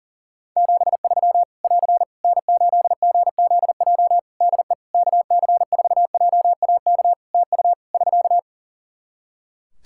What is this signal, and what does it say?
Morse code, heard as a keyed tone, about 40 wpm, 700 Hz. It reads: 73 <AR> N8GZJ DE KC4JAK TU <SK>
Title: Practice QSO - Exchange 4 - Ending - 40wpm